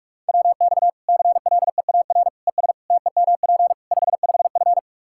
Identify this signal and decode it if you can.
Morse code, heard as a keyed tone, about 46 wpm, 700 Hz. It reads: WX CLEAR ES TEMP 55F
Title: Practice QSO - Exchange 2 - Info - 45wpm